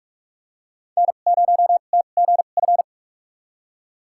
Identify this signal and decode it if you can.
Morse code, heard as a keyed tone, about 48 wpm, spaced as for 33 wpm, 700 Hz. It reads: N0TGF
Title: USA Call Signs - 5 Characters - 2x Character Spacing - 45wpm